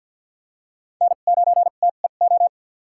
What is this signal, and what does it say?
Morse code, heard as a keyed tone, about 46 wpm, spaced as for 35 wpm, 700 Hz. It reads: N9TEO
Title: USA Call Signs - 2x Character Spacing - 50wpm